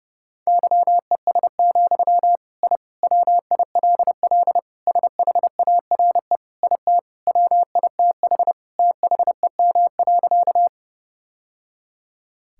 Morse code, heard as a keyed tone, about 31 wpm, 700 Hz. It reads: YES, I WILL SHARE IT WITH THEM.
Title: Sentences from Top 1000 Words - Review Part 6 - 30wpm